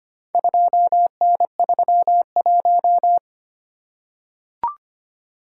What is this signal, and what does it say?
Morse code, heard as a keyed tone, about 26 wpm, 700 Hz. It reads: 2N31
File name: practice / 25wpm / Four Letter-Number Combinations - 25wpm